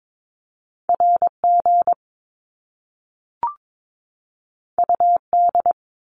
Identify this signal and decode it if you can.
Morse code, heard as a keyed tone, about 22 wpm, 700 Hz. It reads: RG UD
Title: Top Two Letter Combinations in Words 22wpm